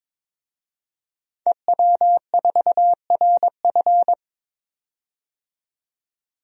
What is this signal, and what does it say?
Morse code, heard as a keyed tone, about 22 wpm, 700 Hz. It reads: EW4RF
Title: DXCC-100 Call Signs - 5 Characters - 22wpm